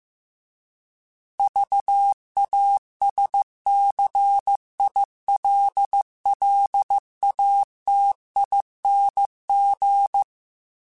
Morse code, heard as a keyed tone, about 15 wpm, 800 Hz. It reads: VASCILLATING